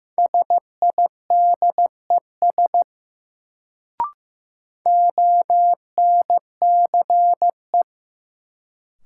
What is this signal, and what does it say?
Morse code, heard as a keyed tone, about 15 wpm, 700 Hz. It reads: SIDES ONCE
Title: Sets of 4 Words - New Word Drill - 15wpm